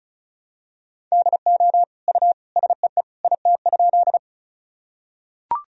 Morse code, heard as a keyed tone, about 34 wpm, 700 Hz. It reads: DO U SEE IT?
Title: Sentences from Top 100 Words and Abbreviations 35wpm